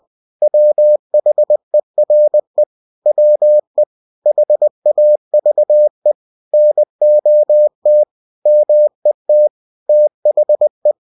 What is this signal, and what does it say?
Morse code, heard as a keyed tone, about 20 wpm, 600 Hz. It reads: WHERE WE HAVE NOT MET THE